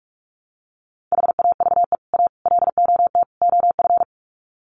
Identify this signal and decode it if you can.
Morse code, heard as a keyed tone, about 44 wpm, 700 Hz. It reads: HAVE A LOT OF
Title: Sets of 4 Words 45wpm